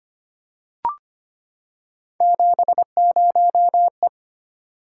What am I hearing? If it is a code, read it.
Morse code, heard as a keyed tone, about 25 wpm, 700 Hz. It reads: 70E
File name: Three Letter-Number Combinations - 25wpm